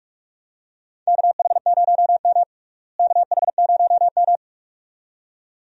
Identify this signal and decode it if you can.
Morse code, heard as a keyed tone, about 43 wpm, 700 Hz. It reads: KH0M KH0M
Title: USA Call Signs - 4 Characters - Repeated 2x - 45wpm